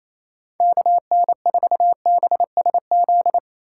Morse code, heard as a keyed tone, about 28 wpm, 700 Hz. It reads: KN4BSZ
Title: USA Call Signs - 6 Characters - 4x Character Spacing - 28wpm